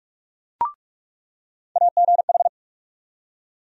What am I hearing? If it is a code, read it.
Morse code, heard as a keyed tone, about 47 wpm, 700 Hz. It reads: AGH
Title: USA Call Sign Postfixes - 3 Letters 45wpm